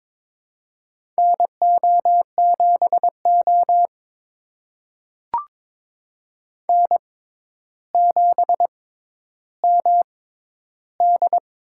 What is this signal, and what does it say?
Morse code, heard as a keyed tone, about 22 wpm, spaced as for 11 wpm, 700 Hz. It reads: NO7O N7MD
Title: USA Call Signs - 4 Characters - 6x Character Spacing - 22wpm